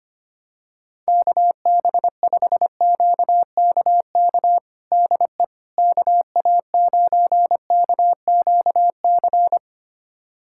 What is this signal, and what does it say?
Morse code, heard as a keyed tone, about 25 wpm, 700 Hz. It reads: KB5QKK DE KA9KQC
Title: USA Callsign DE Callsign - 6 Characters - 25wpm